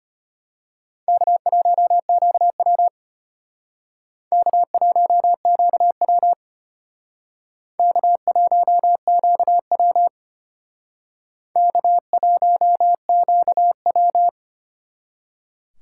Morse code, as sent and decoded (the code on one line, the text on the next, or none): -.- .---- --.- .-- / -.- .---- --.- .-- / -.- .---- --.- .-- / -.- .---- --.- .--
K1QW K1QW K1QW K1QW